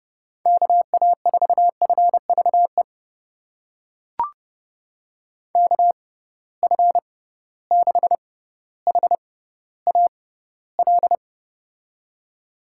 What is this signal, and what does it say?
Morse code, heard as a keyed tone, about 30 wpm, spaced as for 15 wpm, 700 Hz. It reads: KA4FVE KF6HAL
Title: USA Call Signs - 6 Characters - 6x Character Spacing - 30wpm